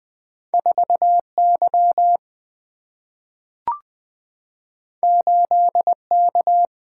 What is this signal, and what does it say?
Morse code, heard as a keyed tone, about 20 wpm, 700 Hz. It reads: 4 Y 8 K
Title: Single-Letter Single-Number 20wpm